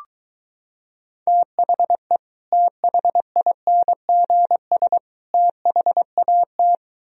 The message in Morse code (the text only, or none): THE THINGS THAT